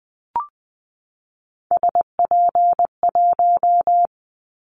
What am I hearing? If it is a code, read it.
Morse code, heard as a keyed tone, about 20 wpm, 700 Hz. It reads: SP1
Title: Callsign Prefixes for Top 25 DXCC - 20wpm